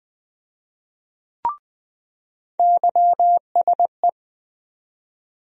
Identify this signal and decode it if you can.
Morse code, heard as a keyed tone, about 20 wpm, 700 Hz. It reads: YSE
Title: USA Call Sign Postfixes - 3 Letters 20wpm